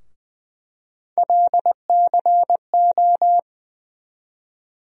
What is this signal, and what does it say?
Morse code, heard as a keyed tone, about 20 wpm, 700 Hz. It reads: LCO
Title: USA Call Sign Postfixes - 3 Letters 20wpm